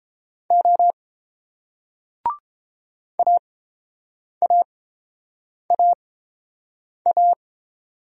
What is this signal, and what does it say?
Morse code, heard as a keyed tone, about 30 wpm, spaced as for 7 wpm, 700 Hz. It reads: O AAAA